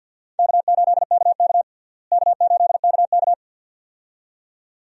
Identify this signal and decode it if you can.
Morse code, heard as a keyed tone, about 52 wpm, 700 Hz. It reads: K8KK K8KK
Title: USA Call Signs - 4 Characters - Repeated 2x - 50wpm